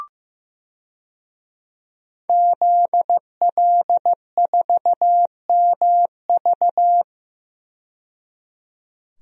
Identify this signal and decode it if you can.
Morse code, heard as a keyed tone, about 15 wpm, 700 Hz. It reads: ZL4MV
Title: DXCC-100 Call Signs - 5 Characters - 15wpm